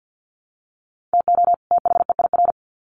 Morse code, heard as a keyed tone, about 50 wpm, 700 Hz. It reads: TO THEIR
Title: Sets of 2 Words 50wpm